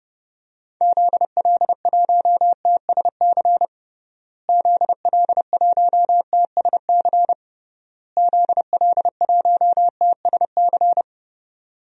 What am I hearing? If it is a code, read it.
Morse code, heard as a keyed tone, about 32 wpm, 700 Hz. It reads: ZL1TSC ZL1TSC ZL1TSC
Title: DXCC-100 Call Signs - 6 Characters - Repeated 3x - 30wpm